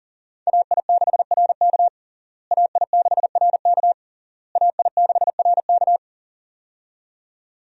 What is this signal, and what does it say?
Morse code, heard as a keyed tone, about 40 wpm, 700 Hz. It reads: AI6RK AI6RK AI6RK
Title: USA Call Signs - 5 Characters - Repeated 3x - 40wpm